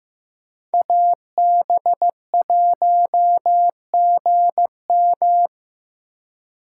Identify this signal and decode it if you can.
Morse code, heard as a keyed tone, about 18 wpm, 700 Hz. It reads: AB1GM
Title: Speed Racing - USA 2x2 Call Signs 15wpm